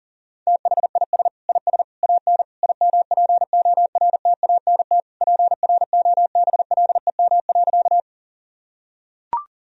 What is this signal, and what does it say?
Morse code, heard as a keyed tone, about 37 wpm, 700 Hz. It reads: THIS IS AN IMPORTANT PROBLEM.
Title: Sentences from Top 1000 Words - Review Part 2 - 40wpm